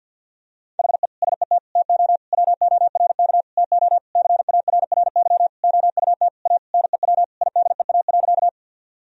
Morse code, heard as a keyed tone, about 48 wpm, 700 Hz. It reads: HE SET TO WORK TO CARRY OUT A NEW IDEA.
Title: Sentences from Top 1000 Words - Review Part 1 - 50wpm